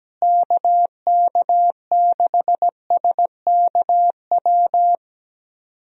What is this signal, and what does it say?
Morse code, heard as a keyed tone, about 17 wpm, 700 Hz. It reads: KK6SKW
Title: USA Call Signs - 6 Characters - Repeated 3x - 17wpm